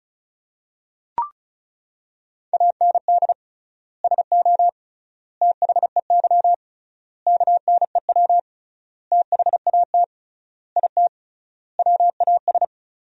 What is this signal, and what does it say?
Morse code, heard as a keyed tone, about 36 wpm, 700 Hz. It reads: AND SO THEY KNEW THAT IT WAS
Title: 3x Word-Spacing - Sentences from Top 400 Words - 35wpm